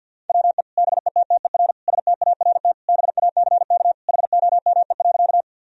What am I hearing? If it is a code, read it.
Morse code, heard as a keyed tone, about 50 wpm, 700 Hz. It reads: WE BETTER START BACK HOME.
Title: Sentences from Top 1000 Words - Review Part 1 - 50wpm